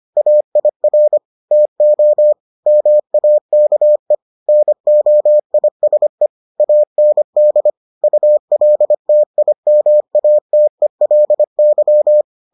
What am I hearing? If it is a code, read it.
Morse code, heard as a keyed tone, about 25 wpm, 600 Hz. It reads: AIR TO MAKE NOISE AND ULTIMATELY